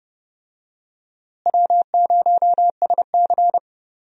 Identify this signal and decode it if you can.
Morse code, heard as a keyed tone, about 30 wpm, 700 Hz. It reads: W0SC
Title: USA Call Signs - 4 Characters - Repeated 3x - 30wpm